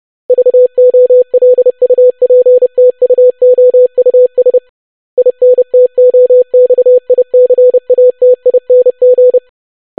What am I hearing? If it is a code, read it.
Morse code, heard as a keyed tone, about 30 wpm, 500 Hz. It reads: VOLUPTUOUS INTOXICATING
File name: longwords-25wpm